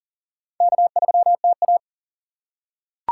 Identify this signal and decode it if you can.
Morse code, heard as a keyed tone, about 40 wpm, 700 Hz. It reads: K3TA
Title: USA Call Signs - 6x Character Spacing - 40wpm